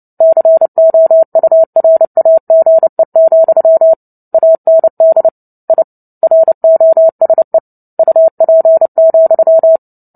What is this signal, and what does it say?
Morse code, heard as a keyed tone, about 30 wpm, 650 Hz. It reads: COURAGE, AND I ROSE UP,
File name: Book Two / 03-Chapter 3 - 29 WPM